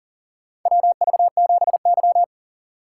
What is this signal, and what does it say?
Morse code, heard as a keyed tone, about 40 wpm, 700 Hz. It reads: WV7Y
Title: USA Call Signs - 4x Character Spacing - 40wpm